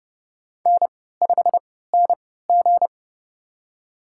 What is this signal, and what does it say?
Morse code, heard as a keyed tone, about 30 wpm, spaced as for 17 wpm, 700 Hz. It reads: N5NG
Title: USA Call Signs - 4 Characters - 3x Character Spacing - 30wpm